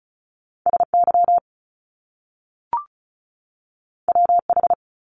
Morse code, heard as a keyed tone, about 36 wpm, 700 Hz. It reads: SY WH
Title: Top Two Letter Combinations in Words 35wpm